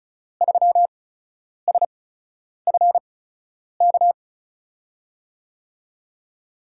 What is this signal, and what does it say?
Morse code, heard as a keyed tone, about 36 wpm, spaced as for 10 wpm, 700 Hz. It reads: 3SFK
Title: USA Call Signs - 6 Characters - 8x Character Spacing - 35wpm